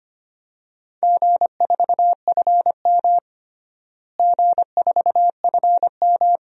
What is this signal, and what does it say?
Morse code, heard as a keyed tone, about 25 wpm, 700 Hz. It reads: G4FM G4FM
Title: DXCC-100 Call Signs - 4 Characters - Repeated 2x - 25wpm